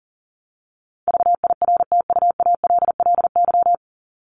Morse code, heard as a keyed tone, about 40 wpm, 700 Hz. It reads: VIRTUALLY
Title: LY Suffix - Top 100 Words with Base Words 40wpm